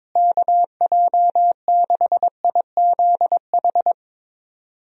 Morse code, heard as a keyed tone, about 22 wpm, 700 Hz. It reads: KJ6IZH
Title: USA Call Signs - 6 Characters - Repeated 2x - 22wpm